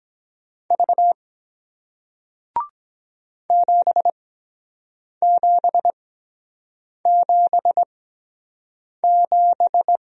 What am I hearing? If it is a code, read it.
Morse code, heard as a keyed tone, about 23 wpm, spaced as for 7 wpm, 700 Hz. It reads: V 7777